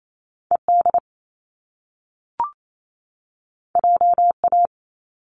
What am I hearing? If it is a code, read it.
Morse code, heard as a keyed tone, about 28 wpm, 700 Hz. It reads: E D J A
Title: Top Two Letter Combinations in Words 28wpm